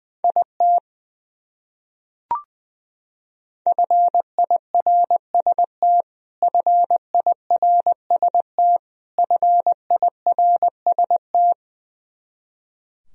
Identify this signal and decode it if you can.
Morse code, heard as a keyed tone, about 21 wpm, 700 Hz. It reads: IT FIRST FIRST FIRST
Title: Top 100 Words Repeated 20wpm